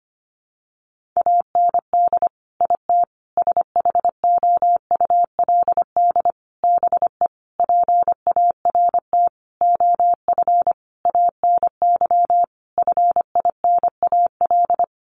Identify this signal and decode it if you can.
Morse code, heard as a keyed tone, about 25 wpm, 700 Hz. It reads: AND IT SHOULD BE PART OF ANY FINAL
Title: Sentences from Top 1500 Words 25wpm